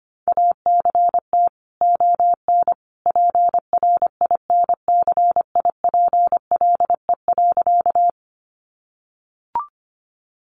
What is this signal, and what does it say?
Morse code, heard as a keyed tone, about 25 wpm, 700 Hz. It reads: ACT ON PRINCIPLE.
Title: Sentences from Top 1400 Words 25wpm